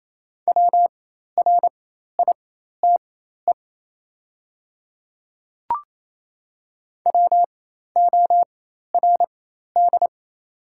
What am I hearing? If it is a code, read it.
Morse code, heard as a keyed tone, about 28 wpm, spaced as for 13 wpm, 700 Hz. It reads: WRITE WORD
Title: 4x Character Spacing - Top 200 Words - 28wpm